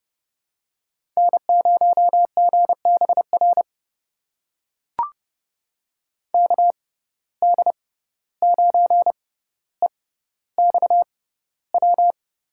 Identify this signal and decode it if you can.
Morse code, heard as a keyed tone, about 30 wpm, spaced as for 15 wpm, 700 Hz. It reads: N0GBR KD9EXW
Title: USA Call Signs - 6x Character Spacing - 30wpm